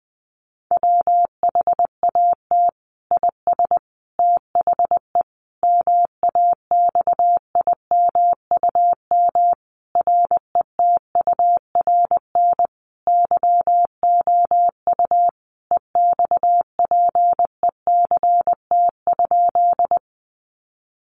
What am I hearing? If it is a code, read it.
Morse code, heard as a keyed tone, about 19 wpm, 700 Hz. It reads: WHAT IS THE MAXIMUM RETURN YOU EXPECT?
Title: Sentences from Top 1300 Words 20wpm